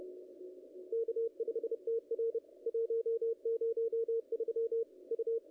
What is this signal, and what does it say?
Morse code, heard as a keyed tone, about 30 wpm, 440 Hz, with radio noise. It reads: K5TR 103 U
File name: contests / ss-cw-k5tr-op-n5rz -2300